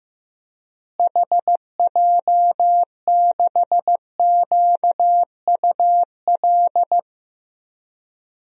Morse code, heard as a keyed tone, about 15 wpm, 700 Hz. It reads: HJ6QUL
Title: DXCC-100 Call Signs - 6 Characters - Repeated 3x - 15wpm